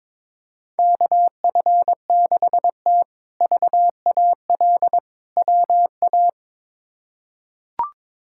Morse code, heard as a keyed tone, about 22 wpm, 700 Hz. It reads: KF6T VAL WA
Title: K1USN SST QSO Practice - 22wpm